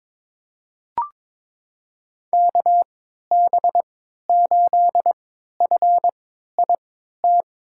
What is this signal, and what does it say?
Morse code, heard as a keyed tone, about 22 wpm, spaced as for 12 wpm, 700 Hz. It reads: KB8FIT
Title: USA Call Signs - 6 Characters - 3x Character Spacing - 22wpm